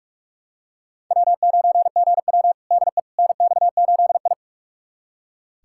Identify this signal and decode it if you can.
Morse code, heard as a keyed tone, about 44 wpm, 690 Hz. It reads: W9GW DE NX8I
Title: USA Callsign DE Callsign - 4 Characters - 45wpm